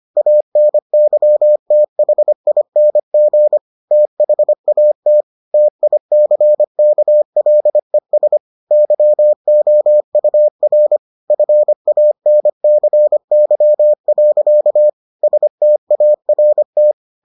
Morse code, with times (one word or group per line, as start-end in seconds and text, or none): 0.2-3.6 ANYTHING
3.9-5.2 THAT
5.5-8.4 TICKLES
8.7-11.0 YOUR
11.3-14.9 FANCY.
15.2-16.9 START